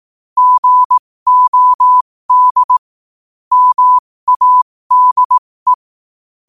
Morse code, heard as a keyed tone, about 18 wpm, spaced as for 15 wpm, 1000 Hz. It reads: GOD MADE
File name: qotdfiles / 15wpm